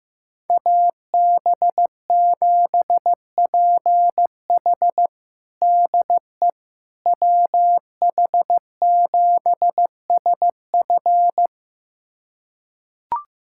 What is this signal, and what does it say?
Morse code, heard as a keyed tone, about 16 wpm, 700 Hz. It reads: AB7PH DE WH7SF
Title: USA Callsign DE Callsign - 5 Characters - 15wpm